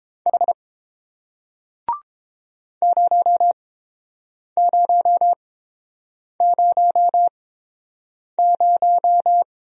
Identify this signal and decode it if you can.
Morse code, heard as a keyed tone, about 30 wpm, spaced as for 7 wpm, 700 Hz. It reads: H 0000